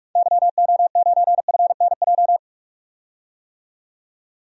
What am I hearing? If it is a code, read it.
Morse code, heard as a keyed tone, about 46 wpm, 700 Hz. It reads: YO9FNJ
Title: DXCC-100 Call Signs - 6 Characters - 45wpm